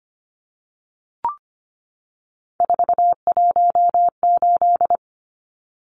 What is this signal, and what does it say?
Morse code, heard as a keyed tone, about 25 wpm, 700 Hz. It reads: 418
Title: Numbers - 3-Digit 25wpm